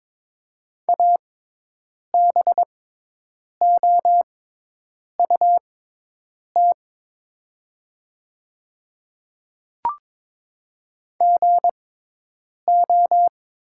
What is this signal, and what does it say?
Morse code, heard as a keyed tone, about 22 wpm, spaced as for 8 wpm, 700 Hz. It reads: ABOUT GO
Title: 6x Character Spacing - Top 100 Words - 22wpm